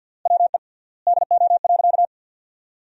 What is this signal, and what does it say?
Morse code, heard as a keyed tone, about 50 wpm, 700 Hz. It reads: WE DO.
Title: 3x Word-Spacing - Sentences from Top 100 Words - 50wpm